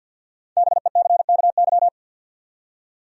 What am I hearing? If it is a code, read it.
Morse code, heard as a keyed tone, about 47 wpm, 700 Hz. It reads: BECKY
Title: USA Common Names - 50wpm